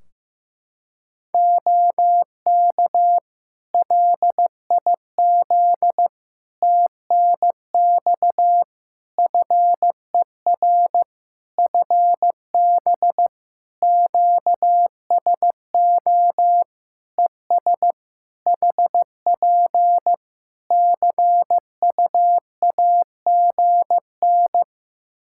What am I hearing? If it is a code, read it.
Morse code, heard as a keyed tone, about 15 wpm, 700 Hz. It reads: OK LIZ TNX FER FB QSO ES HP CUAGN
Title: Practice QSO - Exchange 4 - Ending - 15wpm